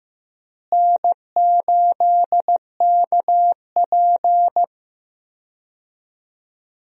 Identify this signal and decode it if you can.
Morse code, heard as a keyed tone, about 15 wpm, 700 Hz. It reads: N8KP
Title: USA Call Signs - 4 Characters - Repeated 2x - 15wpm